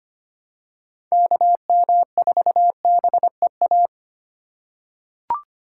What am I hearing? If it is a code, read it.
Morse code, heard as a keyed tone, about 25 wpm, 700 Hz. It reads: KM4BEA
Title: USA Call Signs - 6 Characters - 2x Character Spacing - 25wpm